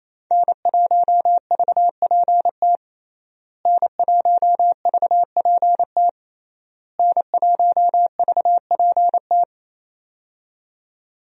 Morse code, heard as a keyed tone, about 27 wpm, 700 Hz. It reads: N1VPT N1VPT N1VPT
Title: USA Call Signs - 5 Characters - Repeated 3x - 28wpm